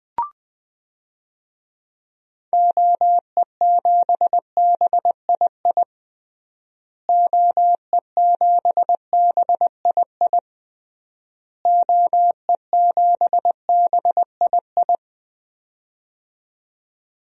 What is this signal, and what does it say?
Morse code, heard as a keyed tone, about 21 wpm, 700 Hz. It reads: OE7BII OE7BII OE7BII
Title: DXCC-100 Call Signs - 6 Characters - Repeated 3x - 20wpm